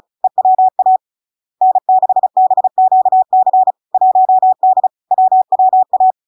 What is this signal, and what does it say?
Morse code, heard as a keyed tone, about 36 wpm, 750 Hz. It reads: EWA N6BQC 1D WWA
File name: LARK-FD-2022-Log-in-CW-35wpm-750-Hz